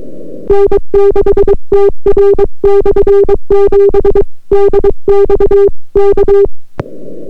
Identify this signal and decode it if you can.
Morse code, heard as a keyed tone, about 23 wpm, 400 Hz. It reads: N6TR/7 DXK